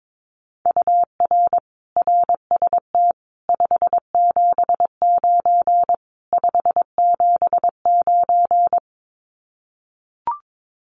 Morse code, heard as a keyed tone, about 22 wpm, 700 Hz. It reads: UR RST 579 579
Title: Common QSO Phrases Encore 22wpm